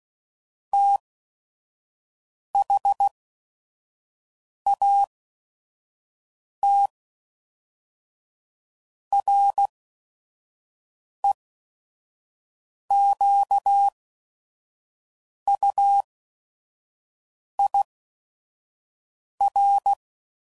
Morse code, heard as a keyed tone, about 16 wpm, spaced as for 5 wpm, 800 Hz. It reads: THAT REQUIR
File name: CW Code Groups @5WPM / Technical Subjects / AND Gate